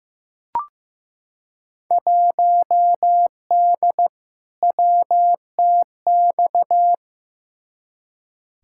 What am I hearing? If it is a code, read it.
Morse code, heard as a keyed tone, about 15 wpm, 700 Hz. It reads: 1D WTX
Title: Field Day Exchanges - 15wpm